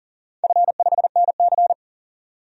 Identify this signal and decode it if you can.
Morse code, heard as a keyed tone, about 39 wpm, 700 Hz. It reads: F5NC
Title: DXCC-100 Call Signs - 4 Characters - 40wpm